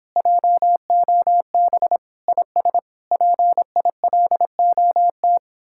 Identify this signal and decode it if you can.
Morse code, heard as a keyed tone, about 29 wpm, 700 Hz. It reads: JOB IS PILOT